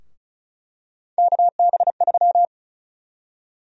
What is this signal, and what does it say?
Morse code, heard as a keyed tone, about 36 wpm, 700 Hz. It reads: KB3
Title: USA Call Sign Prefixes - 35wpm